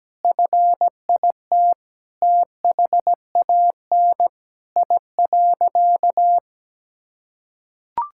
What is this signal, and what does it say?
Morse code, heard as a keyed tone, about 17 wpm, 700 Hz. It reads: FIT THAN I.
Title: Sentences from Top 1000 Words - Review Part 5 - 17wpm